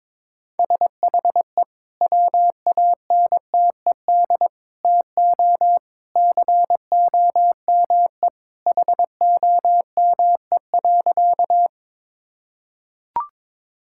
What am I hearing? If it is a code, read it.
Morse code, heard as a keyed tone, about 22 wpm, 700 Hz. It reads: SHE WANTED TO COME HOME.
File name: practice / 22wpm / Sentences with 2-Word Phrases - 22wpm